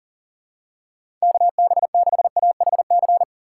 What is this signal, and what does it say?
Morse code, heard as a keyed tone, about 40 wpm, 700 Hz. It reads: KB6AHC
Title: USA Call Signs - 6 Characters - 2x Character Spacing - 40wpm